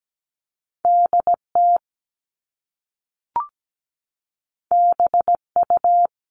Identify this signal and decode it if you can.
Morse code, heard as a keyed tone, about 17 wpm, 700 Hz. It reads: DT BU
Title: All Cut Numbers - 2-Digit 17wpm